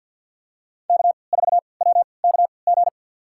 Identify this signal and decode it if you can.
Morse code, heard as a keyed tone, about 48 wpm, spaced as for 28 wpm, 700 Hz. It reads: K4WKG
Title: USA Call Signs - 5 Characters - 3x Character Spacing - 50wpm